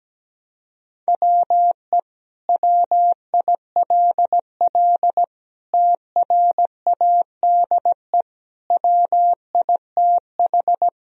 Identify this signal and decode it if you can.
Morse code, heard as a keyed tone, about 17 wpm, 700 Hz. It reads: WE WILL TRADE WITH
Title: Sentences from Top 1000 Words - Review Part 4 - 17wpm